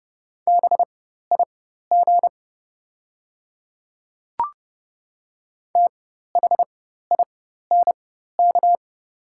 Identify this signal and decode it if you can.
Morse code, heard as a keyed tone, about 30 wpm, spaced as for 14 wpm, 690 Hz. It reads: BIG THINK
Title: 4x Character Spacing - Top 500 Words - 30wpm